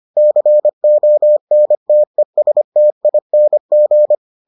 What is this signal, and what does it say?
Morse code, heard as a keyed tone, about 25 wpm, 600 Hz. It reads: CONTESTING